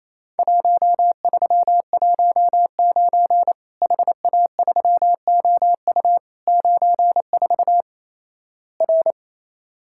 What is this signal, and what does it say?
Morse code, heard as a keyed tone, about 28 wpm, 700 Hz. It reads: 1319 HA3OU 94 R
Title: Worked All Europe DX Contest - QTC Exchange - 28wpm